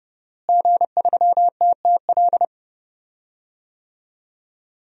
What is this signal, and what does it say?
Morse code, heard as a keyed tone, about 30 wpm, 700 Hz. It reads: G3TTL
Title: DXCC-100 Call Signs - 5 Characters - Repeated 3x - 30wpm